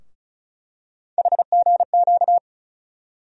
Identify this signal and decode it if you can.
Morse code, heard as a keyed tone, about 36 wpm, 700 Hz. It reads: HGQ
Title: USA Call Sign Postfixes - 3 Letters 35wpm